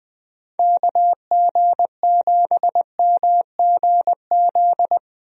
Morse code, heard as a keyed tone, about 20 wpm, 700 Hz. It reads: KG7MGZ
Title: USA Call Signs - 6 Characters - 20wpm